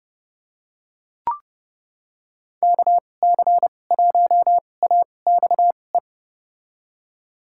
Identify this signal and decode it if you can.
Morse code, heard as a keyed tone, about 31 wpm, spaced as for 22 wpm, 700 Hz. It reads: KC1AXE
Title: USA Call Signs - 2x Character Spacing - 30wpm